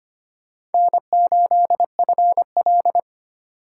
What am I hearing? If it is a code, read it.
Morse code, heard as a keyed tone, about 31 wpm, 700 Hz. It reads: N8FL
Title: Speed Racing - USA Call Signs 25wpm